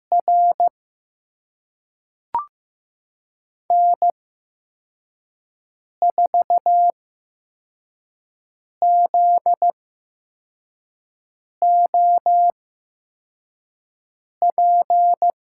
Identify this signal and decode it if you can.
Morse code, heard as a keyed tone, about 15 wpm, spaced as for 4 wpm, 700 Hz. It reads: R N4ZOP
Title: USA Call Signs - 8x Character Spacing - 15wpm